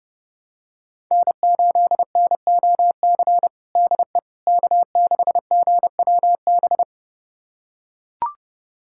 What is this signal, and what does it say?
Morse code, heard as a keyed tone, about 30 wpm, 700 Hz. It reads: N8NOC DE K6GWB
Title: USA Callsign DE Callsign - 5 Characters - 30wpm